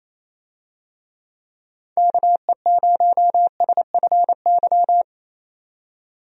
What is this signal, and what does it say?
Morse code, heard as a keyed tone, about 28 wpm, 700 Hz. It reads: KE0SFY
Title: USA Call Signs - 6 Characters - 28wpm